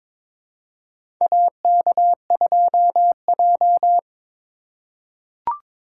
Morse code, heard as a keyed tone, about 22 wpm, 700 Hz. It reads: AK2J
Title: USA Call Signs - 4x Character Spacing - 22wpm